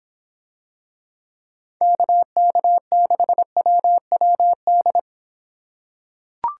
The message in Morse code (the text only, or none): KK6WWD